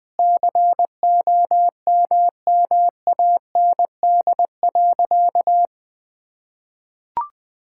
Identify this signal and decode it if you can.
Morse code, heard as a keyed tone, about 20 wpm, 700 Hz. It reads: COMMAND.
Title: Sentences from Top 1600 Words 20wpm